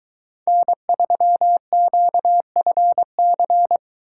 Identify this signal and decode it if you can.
Morse code, heard as a keyed tone, about 23 wpm, 700 Hz. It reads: N3QFC